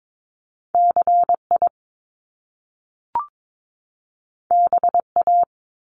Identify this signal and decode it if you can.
Morse code, heard as a keyed tone, about 22 wpm, 700 Hz. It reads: CI BA
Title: Top Two Letter Combinations in Words 22wpm